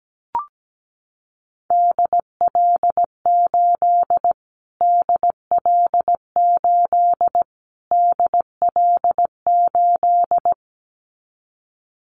Morse code, heard as a keyed tone, about 17 wpm, 700 Hz. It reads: DL8 DL8 DL8
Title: Callsign Prefixes for Top 25 DXCC - Repeated 3x - 17wpm